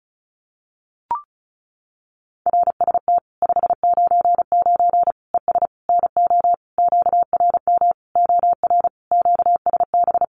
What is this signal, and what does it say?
Morse code, heard as a keyed tone, about 34 wpm, 700 Hz. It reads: RST 599 ES NO QRM OR QSB
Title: Common QSO Phrases 35wpm